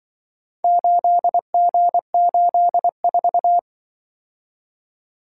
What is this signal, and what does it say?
Morse code, heard as a keyed tone, about 24 wpm, 700 Hz. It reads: 8G84
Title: Speed-Racing - Four Letter-Number Combinations - 20wpm